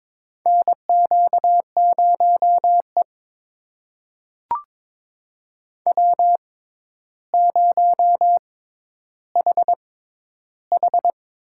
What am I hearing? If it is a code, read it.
Morse code, heard as a keyed tone, about 22 wpm, spaced as for 11 wpm, 700 Hz. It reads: NQ0E W0HH
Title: USA Call Signs - 4 Characters - 6x Character Spacing - 22wpm